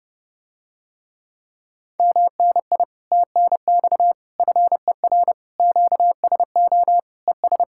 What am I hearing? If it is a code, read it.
Morse code, heard as a keyed tone, about 30 wpm, 700 Hz. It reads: MNI TNX FER QSO ES